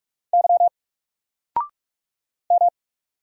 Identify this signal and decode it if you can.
Morse code, heard as a keyed tone, about 46 wpm, 700 Hz. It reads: Y M
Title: Single Letter-Number 45wpm